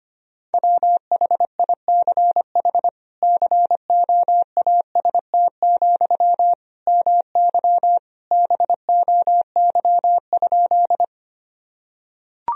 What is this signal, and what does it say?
Morse code, heard as a keyed tone, about 25 wpm, 700 Hz. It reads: WHICH COAST, MY BOY?
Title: Sentences from Top 1000 Words - Review Part 4 - 25wpm